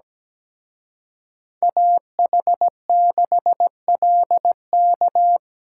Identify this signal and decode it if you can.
Morse code, heard as a keyed tone, about 20 wpm, 700 Hz. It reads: AH6LK
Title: Speed Racing - USA 2x2 Call Signs 17wpm